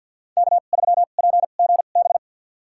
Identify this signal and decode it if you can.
Morse code, heard as a keyed tone, about 49 wpm, spaced as for 36 wpm, 700 Hz. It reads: K3PGB
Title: USA Call Signs - 5 Characters - 2x Character Spacing - 50wpm